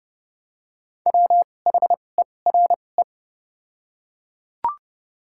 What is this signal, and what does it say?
Morse code, heard as a keyed tone, about 30 wpm, spaced as for 22 wpm, 700 Hz. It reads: WHERE
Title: 2x Character Spacing - Top 200 Words - 30wpm